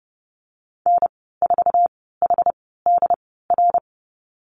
Morse code, heard as a keyed tone, about 30 wpm, spaced as for 17 wpm, 700 Hz. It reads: N4HDR
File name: Lesson 15g - Number 5 - Call Signs 30wpm